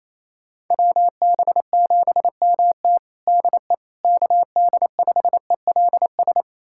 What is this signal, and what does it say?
Morse code, heard as a keyed tone, about 28 wpm, 700 Hz. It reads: WB7MT DE KD5ELS
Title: USA Callsign DE Callsign 28wpm